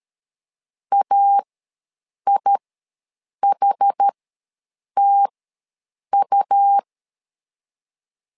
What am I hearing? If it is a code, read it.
Morse code, heard as a keyed tone, about 13 wpm, spaced as for 7 wpm, 800 Hz. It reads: AIHTU
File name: LEZIONI / 3 MWRO / 3 MWR ws 12wpm 1809